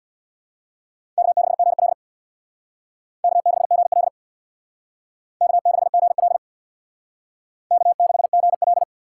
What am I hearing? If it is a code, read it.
Morse code, heard as a keyed tone, about 61 wpm, 700 Hz. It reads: K6GL K6GL K6GL K6GL
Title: Speed Racing - USA 1x2 Call Signs 50wpm